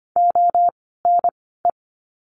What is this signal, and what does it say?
Morse code, heard as a keyed tone, about 25 wpm, 700 Hz. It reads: O N E
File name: course / 25:10wpm / 094 Lesson 16f - Letter c - Words 25wpm